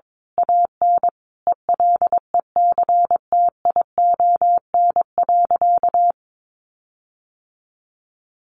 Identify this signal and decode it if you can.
Morse code, heard as a keyed tone, about 22 wpm, 700 Hz. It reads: AN ELECTION.
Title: Sentences from Top 1500 Words 22wpm